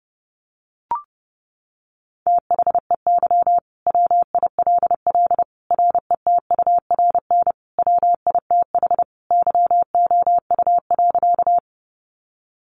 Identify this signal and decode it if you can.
Morse code, heard as a keyed tone, about 30 wpm, 700 Hz. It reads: THEY WILL RETURN WITH YOU.
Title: Sentences from Top 800 Words 30wpm